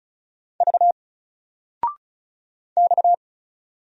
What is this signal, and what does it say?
Morse code, heard as a keyed tone, about 36 wpm, 700 Hz. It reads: V X
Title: Single Letter-Number 35wpm